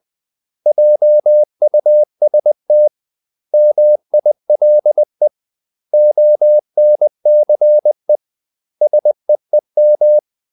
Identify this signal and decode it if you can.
Morse code, heard as a keyed tone, about 20 wpm, 600 Hz. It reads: JUST MILE ONCE SEEM
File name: CW / Practice / 500MostCommonEnglishWords20